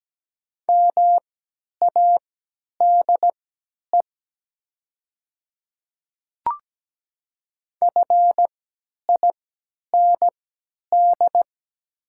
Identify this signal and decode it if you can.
Morse code, heard as a keyed tone, about 17 wpm, spaced as for 10 wpm, 690 Hz. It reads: MADE FIND
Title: 3x Character Spacing - Top 200 Words - 17wpm